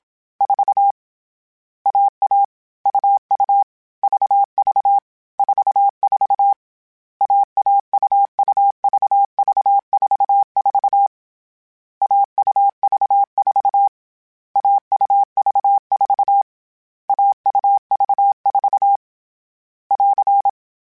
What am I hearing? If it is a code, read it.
Morse code, heard as a keyed tone, about 27 wpm, 800 Hz. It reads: 4 AA UU VV 44 AA UU VV 44 AUV4 AUV4 AUV4 <AR>
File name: Difficult Combos / Difficult Combos-mp3s / AUV AUV4 @25